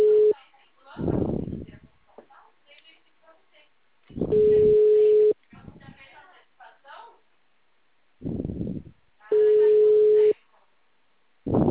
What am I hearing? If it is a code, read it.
Ringback tone (Europe, 425 Hz), with speech.
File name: out- -2015- - - .18431